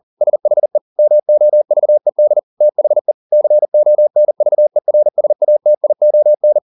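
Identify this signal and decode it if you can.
Morse code, heard as a keyed tone, about 40 wpm, 600 Hz. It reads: SHE MOVED THE CONVERSATION
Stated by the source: Morse code at 40 wpm, practice text, CW med Gnister.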